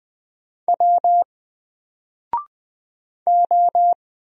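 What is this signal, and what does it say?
Morse code, heard as a keyed tone, about 20 wpm, 700 Hz. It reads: W O
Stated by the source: Single letter and number Morse drill at 20 wpm, from N4LCD.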